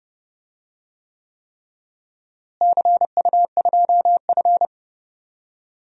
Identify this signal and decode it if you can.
Morse code, heard as a keyed tone, about 30 wpm, 700 Hz. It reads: CU2F
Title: DXCC-100 Call Signs - 4 Characters - Repeated 3x - 30wpm